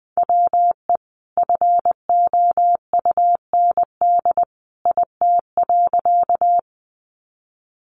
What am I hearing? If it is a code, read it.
Morse code, heard as a keyed tone, about 20 wpm, 700 Hz. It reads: WE FOUND IT.
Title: Sentences from Top 300 Words 20wpm